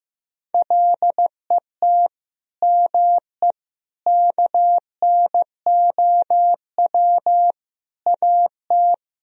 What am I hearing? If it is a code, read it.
Morse code, heard as a keyed tone, about 16 wpm, 700 Hz. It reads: LET ME KNOW AT
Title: Sentences from Top 300 Words and Abbreviations 15wpm